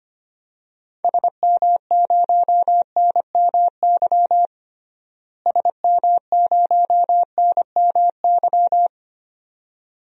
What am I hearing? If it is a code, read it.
Morse code, heard as a keyed tone, about 25 wpm, 700 Hz. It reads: SM0NMY SM0NMY
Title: DXCC-100 Call Signs - 6 Characters - Repeated 2x - 25wpm